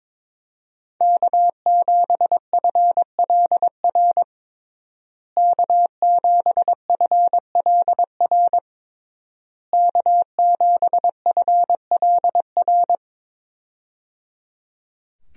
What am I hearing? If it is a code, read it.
Morse code, heard as a keyed tone, about 21 wpm, 700 Hz. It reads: K7FLR K7FLR K7FLR
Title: USA Call Signs - 5 Characters - Repeated 3x - 22wpm